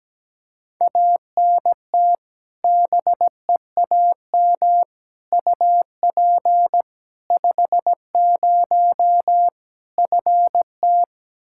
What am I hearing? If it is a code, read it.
Morse code, heard as a keyed tone, about 17 wpm, 700 Hz. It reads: ANT BEAM UP 50 FT
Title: Practice QSO - Exchange 2 - Info - 17wpm